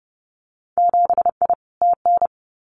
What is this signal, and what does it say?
Morse code, heard as a keyed tone, about 30 wpm, 700 Hz. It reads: 7I TN
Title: Winter Field Day Exchanges Encore 30wpm